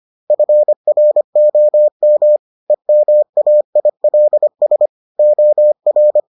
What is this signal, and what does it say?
Morse code, heard as a keyed tone, about 25 wpm, 600 Hz. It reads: FROM EMAILS OR